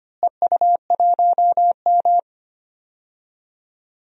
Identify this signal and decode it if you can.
Morse code, heard as a keyed tone, about 25 wpm, 700 Hz. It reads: EU1M
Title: DXCC-100 Call Signs - 4 Characters - Repeated 2x - 25wpm